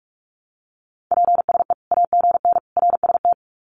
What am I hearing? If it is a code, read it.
Morse code, heard as a keyed tone, about 45 wpm, 700 Hz. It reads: PSE AGN RST
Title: Common QSO Phrases 45wpm